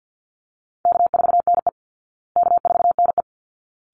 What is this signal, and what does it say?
Morse code, heard as a keyed tone, about 48 wpm, 700 Hz. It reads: K4NE K4NE
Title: Easy Call Signs Repeated 2x - 50wpm